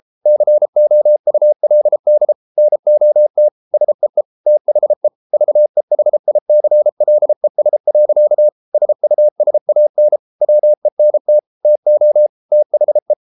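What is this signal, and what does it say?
Morse code, heard as a keyed tone, about 33 wpm, 600 Hz. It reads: COULD NOT SEE THE VEHICLES. SUSAN WENT TO THE
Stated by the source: Morse code practice at 32 wpm from CW med Gnister, reading Ken Follett's novel Never in English.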